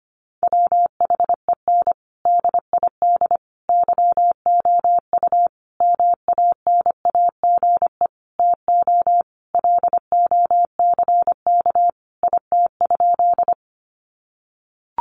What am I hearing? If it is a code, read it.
Morse code, heard as a keyed tone, about 25 wpm, 700 Hz. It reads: WHEN DID YOU MANAGE TO LOCK IT?
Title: Sentences from Top 1400 Words 25wpm